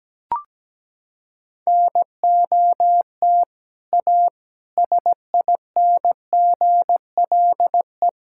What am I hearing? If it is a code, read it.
Morse code, heard as a keyed tone, about 17 wpm, 700 Hz. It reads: NOT A SINGLE
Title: Sentences from Top 1000 Words - Review Part 5 - 17wpm